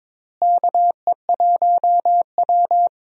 Morse code, heard as a keyed tone, about 22 wpm, 700 Hz. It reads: KE1W
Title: USA Call Signs - 4 Characters - Repeated 3x - 22wpm